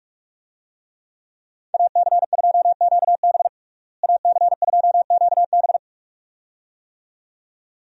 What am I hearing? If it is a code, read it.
Morse code, heard as a keyed tone, about 46 wpm, 700 Hz. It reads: AC2QB AC2QB
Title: USA Call Signs - 5 Characters - Repeated 2x - 45wpm